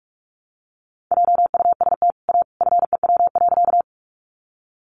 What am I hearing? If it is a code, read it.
Morse code, heard as a keyed tone, about 44 wpm, 700 Hz. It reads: JUST A FEW.
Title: Sentences from Top 300 Words 45wpm